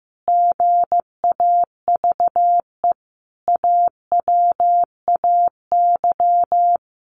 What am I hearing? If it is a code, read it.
Morse code, heard as a keyed tone, about 15 wpm, 700 Hz. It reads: GAVE AWAY
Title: Sentences from Top 1300 Words 15wpm